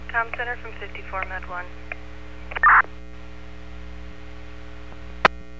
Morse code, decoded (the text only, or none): none